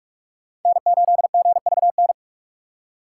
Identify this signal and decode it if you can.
Morse code, heard as a keyed tone, about 46 wpm, 700 Hz. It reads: N8GVN
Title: USA Call Signs - 5 Characters - 8x Character Spacing - 45wpm